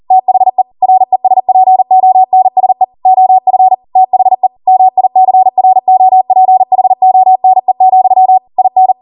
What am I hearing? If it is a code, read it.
Morse code, heard as a keyed tone, about 40 wpm, 750 Hz. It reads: THE RESPONSE OF THE MICROPHONE, IN